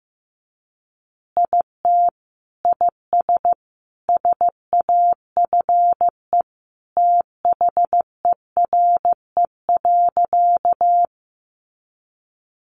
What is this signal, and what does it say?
Morse code, heard as a keyed tone, about 15 wpm, 700 Hz. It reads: IT IS SAFE THERE.
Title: Easy Phrases 15wpm